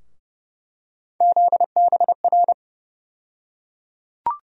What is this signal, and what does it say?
Morse code, heard as a keyed tone, about 30 wpm, 700 Hz. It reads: ZBR
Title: USA Call Sign Postfixes - 3 Letters 30wpm